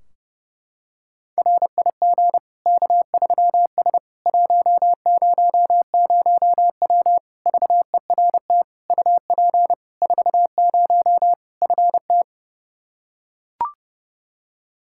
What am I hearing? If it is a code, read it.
Morse code, heard as a keyed tone, about 30 wpm, 700 Hz. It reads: RIG K3S 100W VERT UP 40 FT
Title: K1USN SST QSO Practice 30wpm